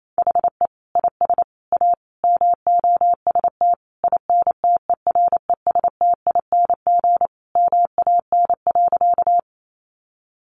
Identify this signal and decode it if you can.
Morse code, heard as a keyed tone, about 28 wpm, 700 Hz. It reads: HE IS A MOST INTERESTING MAN.
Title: Sentences from Top 1000 Words 28wpm